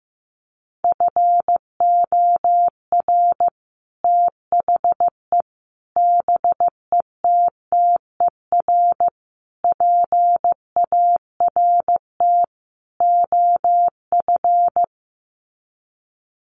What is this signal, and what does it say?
Morse code, heard as a keyed tone, about 15 wpm, 700 Hz. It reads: FOR THE BETTER PART OF
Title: Sets of 5 Words Encore2 15wpm